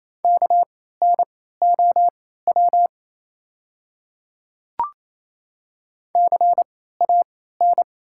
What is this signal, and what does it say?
Morse code, heard as a keyed tone, about 28 wpm, spaced as for 16 wpm, 690 Hz. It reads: KNOW CAN
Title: 3x Character Spacing - Top 100 Words - 28wpm